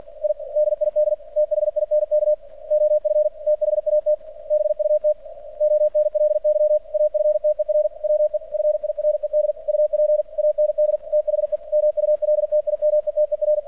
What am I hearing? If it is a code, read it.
Morse code, heard as a keyed tone, about 48 wpm, 600 Hz. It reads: I DID THINK OF THAT BUT ONLY AFTER WE FIRED UP AND THE MULTIMETER
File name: archived / CRSnet